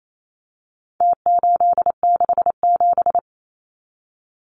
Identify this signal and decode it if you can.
Morse code, heard as a keyed tone, about 28 wpm, 700 Hz. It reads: T867
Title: Numbers - Contest Style - 4-Digit 28wpm